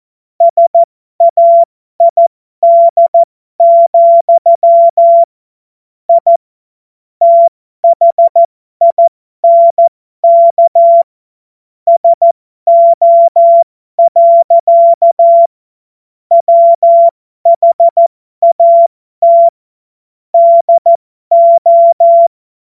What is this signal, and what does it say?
Morse code, heard as a keyed tone, about 14 wpm, spaced as for 12 wpm, 680 Hz. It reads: SAID, I THINK SO. WHAT DO
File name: Morsekurs / Witze